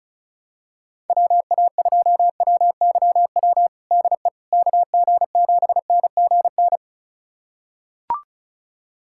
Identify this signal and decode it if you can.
Morse code, heard as a keyed tone, about 36 wpm, 700 Hz. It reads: WA2WYW DE KG7NGN
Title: USA Callsign DE Callsign - 6 Characters - 35wpm